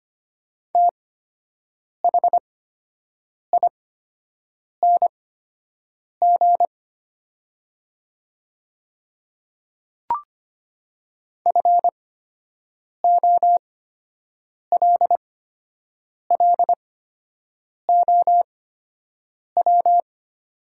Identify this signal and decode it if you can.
Morse code, heard as a keyed tone, about 25 wpm, spaced as for 7 wpm, 700 Hz. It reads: THING FOLLOW
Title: 8x Character Spacing - Top 300 Words - 25wpm